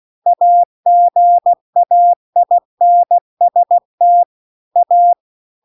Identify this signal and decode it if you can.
Morse code, heard as a keyed tone, about 16 wpm, 700 Hz. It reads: AGAINST A